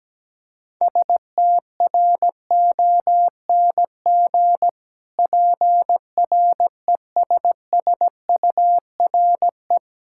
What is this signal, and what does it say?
Morse code, heard as a keyed tone, about 17 wpm, 700 Hz. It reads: STRONG PRESSURE
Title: Sentences from Top 1000 Words - Review Part 6 - 17wpm